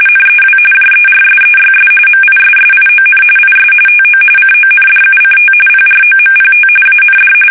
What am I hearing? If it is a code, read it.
RTTY, about 45.45 baud. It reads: NEW JERSEYS MISSION IS TO CONDUCT PROMPT AND
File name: fsk850 / battleshipnj